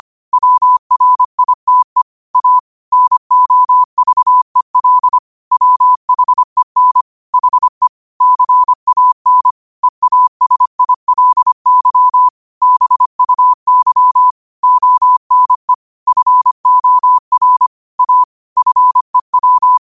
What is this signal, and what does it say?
Morse code, heard as a keyed tone, about 25 wpm, 1000 Hz. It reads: WRITE A NOVEL WHEN HE CAN EASILY BUY ONE FOR A FEW
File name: qotdfiles / 25wpm